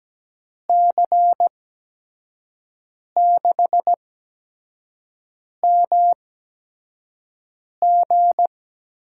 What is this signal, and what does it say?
Morse code, heard as a keyed tone, about 17 wpm, spaced as for 5 wpm, 700 Hz. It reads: C6MG
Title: USA Call Signs - 5 Characters - 8x Character Spacing - 17wpm